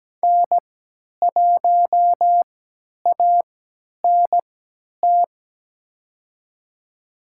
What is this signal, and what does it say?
Morse code, heard as a keyed tone, about 17 wpm, spaced as for 10 wpm, 700 Hz. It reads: N1ANT
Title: USA Call Signs - 5 Characters - 3x Character Spacing - 17wpm